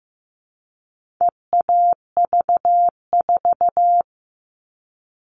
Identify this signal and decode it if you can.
Morse code, heard as a keyed tone, about 15 wpm, 700 Hz. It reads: EAV4
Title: All Cut Numbers - 4-Digit 15wpm